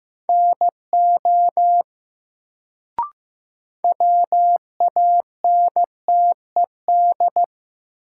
Morse code, heard as a keyed tone, about 15 wpm, 700 Hz. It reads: NO WANTED
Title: Sets of 2 Words - New Word Drill - 15wpm